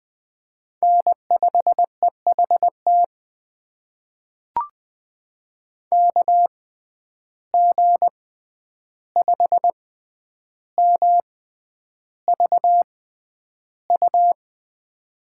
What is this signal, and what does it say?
Morse code, heard as a keyed tone, about 20 wpm, spaced as for 10 wpm, 700 Hz. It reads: N5EHT KG5MVU
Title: USA Call Signs - 6x Character Spacing - 20wpm